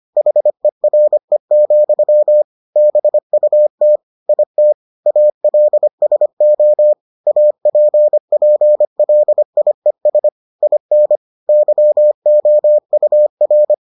Morse code, heard as a keyed tone, about 25 wpm, 600 Hz. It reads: HERE, BUT IT ALSO APPLIES IN YOUR